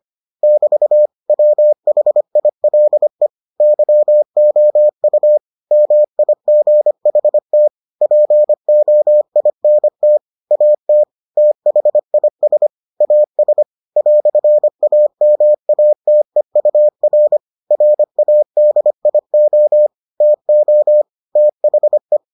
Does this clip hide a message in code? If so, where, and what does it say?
Yes: Morse code from 0.4 to 22.2 s, heard as a keyed tone, about 25 wpm, 600 Hz. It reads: <BT> WHILE YOU MIGHT POINT AT THIS AS "AMATEUR RADIO TO THE